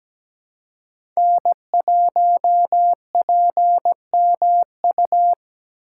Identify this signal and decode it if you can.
Morse code, heard as a keyed tone, about 17 wpm, 700 Hz. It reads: N1PMU
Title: USA Call Signs - 5 Characters - 3x Character Spacing - 17wpm